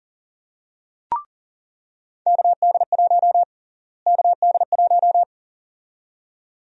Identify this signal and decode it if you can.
Morse code, heard as a keyed tone, about 37 wpm, 700 Hz. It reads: KD1 KD1
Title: USA Call Sign Prefixes - Repeated 2x - 40wpm